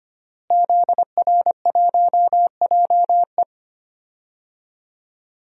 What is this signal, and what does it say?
Morse code, heard as a keyed tone, about 25 wpm, 700 Hz. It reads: ZR1JE
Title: DXCC-100 Call Signs - 5 Characters - Repeated 3x - 25wpm